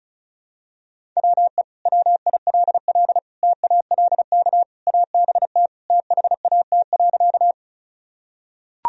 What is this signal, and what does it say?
Morse code, heard as a keyed tone, about 36 wpm, 700 Hz. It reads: WE WILL TALK ABT THAT.
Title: Sentences from Top 400 Words and Abbreviations 35wpm